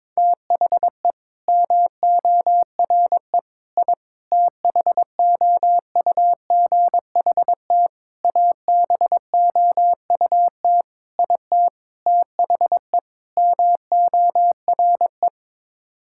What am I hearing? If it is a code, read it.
Morse code, heard as a keyed tone, about 22 wpm, 700 Hz. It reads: THE MORE I THOUGHT ABOUT IT THE MORE
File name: practice / 22wpm / Sets of 8 Words - 22wpm